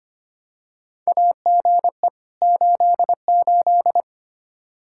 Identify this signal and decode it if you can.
Morse code, heard as a keyed tone, about 25 wpm, 700 Hz. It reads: AGE 88
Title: Practice QSO - Exchange 3 - More Info - 25wpm